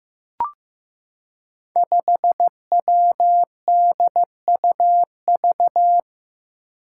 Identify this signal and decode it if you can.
Morse code, heard as a keyed tone, about 15 wpm, 700 Hz. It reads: 5WDUV
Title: Five Challenging Letter-Number Combinations 15wpm